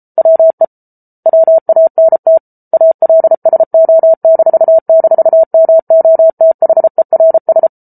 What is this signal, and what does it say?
Morse code, heard as a keyed tone, about 34 wpm, 650 Hz. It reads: WE WANT ALSO--MOTHERS